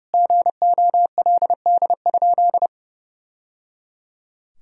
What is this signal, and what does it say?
Morse code, heard as a keyed tone, about 30 wpm, 700 Hz. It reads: GOLD?
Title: Sentences from Top 1000 Words - Review Part 3 - 30wpm